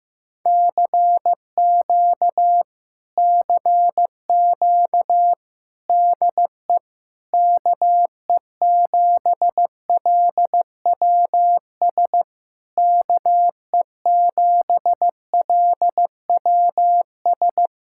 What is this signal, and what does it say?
Morse code, heard as a keyed tone, about 15 wpm, 700 Hz. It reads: CQ CQ DE KE7LWS KE7LWS
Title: Speed-Racing Common QSO Phrases Encore - 15wpm